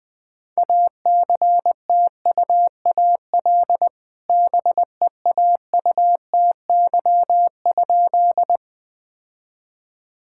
Morse code, heard as a keyed tone, about 20 wpm, 700 Hz. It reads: ACTUAL BEAUTY?
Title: Sentences from Top 1000 Words - Review Part 5 - 20wpm